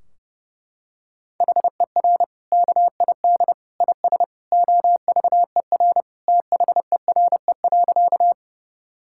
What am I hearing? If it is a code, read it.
Morse code, heard as a keyed tone, about 28 wpm, 700 Hz. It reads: HER KID IS OVER THERE.
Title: Sentences from Top 1600 Words 30wpm